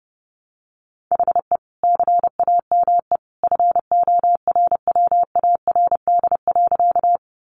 Morse code, heard as a keyed tone, about 30 wpm, 700 Hz. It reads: HE CAME FORWARD.
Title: Sentences from Top 1300 Words 30wpm